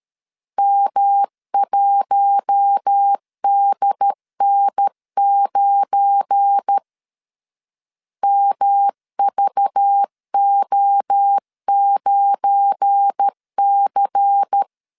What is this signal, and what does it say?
Morse code, heard as a keyed tone, about 13 wpm, 800 Hz. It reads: M1DN9 MVO9C
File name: LEZIONI / 7 GF9Z / 7 GF9 12wpm 2319